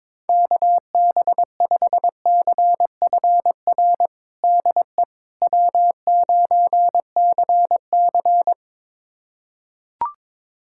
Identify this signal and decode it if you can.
Morse code, heard as a keyed tone, about 22 wpm, 700 Hz. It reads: KB5CFR DE W9CC
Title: USA Callsign DE Callsign 22wpm